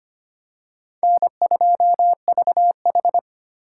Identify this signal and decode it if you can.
Morse code, heard as a keyed tone, about 25 wpm, 700 Hz. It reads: N2VH
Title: USA Call Signs - 4 Characters - 3x Character Spacing - 25wpm